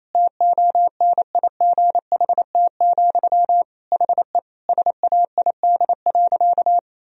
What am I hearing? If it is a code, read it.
Morse code, heard as a keyed tone, about 27 wpm, 700 Hz. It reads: TONIGHT, HE SAID.
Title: Sentences from Top 1000 Words - Review Part 2 - 28wpm